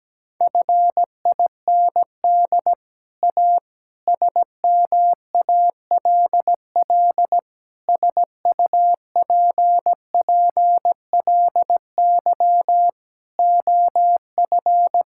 Morse code, heard as a keyed tone, about 17 wpm, 700 Hz. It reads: FIND A SMALL SUPPLY OF
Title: Sentences from Top 1000 Words - Review Part 5 - 17wpm